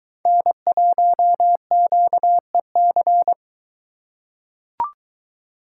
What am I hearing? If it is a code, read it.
Morse code, heard as a keyed tone, about 23 wpm, 700 Hz. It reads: N1QEC